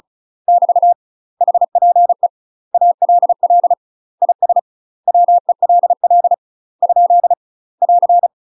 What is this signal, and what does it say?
Morse code, heard as a keyed tone, about 36 wpm, 700 Hz. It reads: <BT> HPE ALL IS WELL ? <AR>
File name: UK MASTER QSOs TRAINING / 35WPM / UK MASTER QSOs 0026